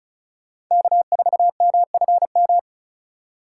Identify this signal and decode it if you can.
Morse code, heard as a keyed tone, about 44 wpm, 700 Hz. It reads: K4MFM